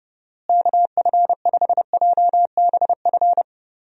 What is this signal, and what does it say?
Morse code, heard as a keyed tone, about 30 wpm, 700 Hz. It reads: KF5JBF
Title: USA Call Signs - 6 Characters - Repeated 2x - 30wpm